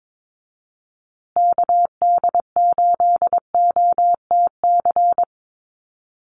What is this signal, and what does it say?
Morse code, heard as a keyed tone, about 22 wpm, 700 Hz. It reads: KD8OTC
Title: USA Call Signs Repeated 3x 22wpm